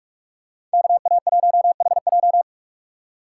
Morse code, heard as a keyed tone, about 47 wpm, 700 Hz. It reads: KA1HJ